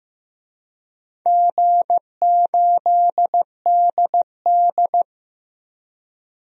Morse code, heard as a keyed tone, about 15 wpm, 700 Hz. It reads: G8DD
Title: DXCC-100 Call Signs - 4 Characters - 15wpm